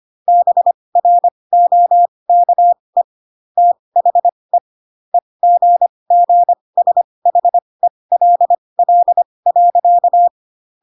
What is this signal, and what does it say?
Morse code, heard as a keyed tone, about 25 wpm, spaced as for 20 wpm, 700 Hz. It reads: BROKE THE EGGSHELL.